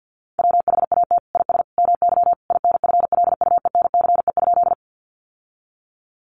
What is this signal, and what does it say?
Morse code, heard as a keyed tone, about 50 wpm, 700 Hz. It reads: WHAT IS MY INFLUENCE?
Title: Sentences from Top 1100 Words 50wpm